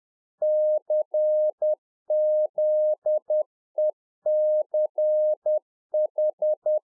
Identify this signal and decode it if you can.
Morse code, heard as a keyed tone, about 10 wpm, 610 Hz. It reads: CZECH
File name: Czech Army Morse Keys 10WPM